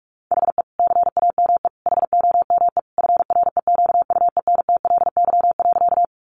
Morse code, heard as a keyed tone, about 44 wpm, 700 Hz. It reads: HE CAME HOME FREQUENTLY.
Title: Sentences from Top 1300 Words 45wpm